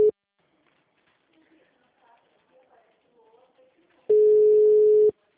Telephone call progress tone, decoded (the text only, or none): ringback tone